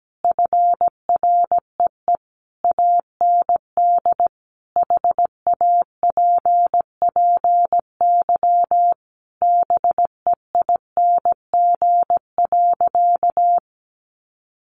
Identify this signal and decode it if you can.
Morse code, heard as a keyed tone, about 17 wpm, 700 Hz. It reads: FREE AND HAPPY BEING.
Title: Sentences from Top 1000 Words 17wpm